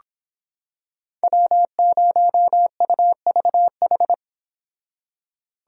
Morse code, heard as a keyed tone, about 26 wpm, 700 Hz. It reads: W0UVH